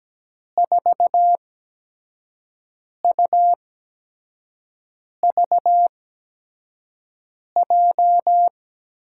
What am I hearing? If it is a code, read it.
Morse code, heard as a keyed tone, about 17 wpm, spaced as for 5 wpm, 700 Hz. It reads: 4UVJ
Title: USA Call Signs - 6 Characters - 8x Character Spacing - 17wpm